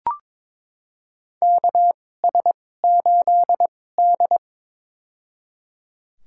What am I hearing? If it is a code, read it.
Morse code, heard as a keyed tone, about 22 wpm, spaced as for 16 wpm, 700 Hz. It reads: KS8D
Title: USA Call Signs - 4 Characters - 2x Character Spacing - 22wpm